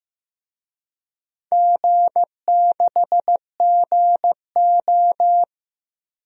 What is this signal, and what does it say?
Morse code, heard as a keyed tone, about 15 wpm, 700 Hz. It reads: G6GO
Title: DXCC-100 Call Signs - 4 Characters - 15wpm